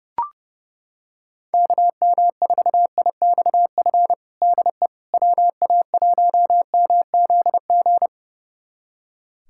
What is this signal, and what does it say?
Morse code, heard as a keyed tone, about 31 wpm, 700 Hz. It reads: KM4IXF DE WA1MZG
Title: USA Callsign DE Callsign 30wpm